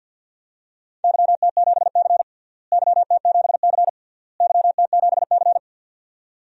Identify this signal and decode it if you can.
Morse code, heard as a keyed tone, about 49 wpm, 700 Hz. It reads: YT7C YT7C YT7C
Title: DXCC-100 Call Signs - 4 Characters - Repeated 3x - 50wpm